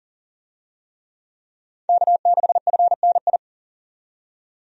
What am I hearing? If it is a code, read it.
Morse code, heard as a keyed tone, about 37 wpm, 700 Hz. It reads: K6FNI
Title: USA Call Signs - 5 Characters - 40wpm